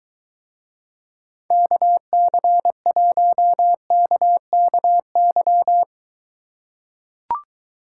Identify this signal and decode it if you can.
Morse code, heard as a keyed tone, about 24 wpm, 700 Hz. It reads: KC1KKY